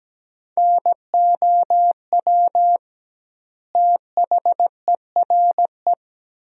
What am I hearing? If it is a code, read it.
Morse code, heard as a keyed tone, about 17 wpm, 700 Hz. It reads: NOW THERE
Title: 2x Word-Spacing - Sentences from Top 100 Words - 17wpm